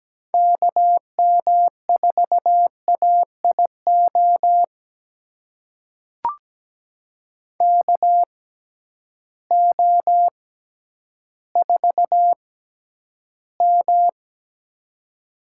Morse code, heard as a keyed tone, about 17 wpm, spaced as for 10 wpm, 700 Hz. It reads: KM4AIO KO4M
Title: USA Call Signs - 6x Character Spacing - 17wpm